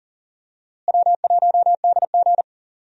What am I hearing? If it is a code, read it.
Morse code, heard as a keyed tone, about 40 wpm, 700 Hz. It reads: W1DG
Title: USA Call Signs - 4 Characters - Repeated 3x - 40wpm